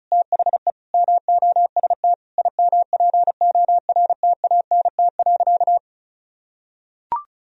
Morse code, heard as a keyed tone, about 36 wpm, 700 Hz. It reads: THE MOST IMPORTANT.
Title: Sentences from Top 1000 Words - Review Part 2 - 35wpm